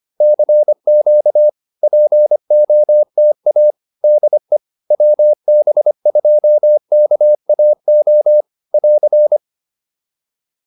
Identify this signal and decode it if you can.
Morse code, heard as a keyed tone, about 25 wpm, 600 Hz. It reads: CQ POTA DE WB2KAO <AR>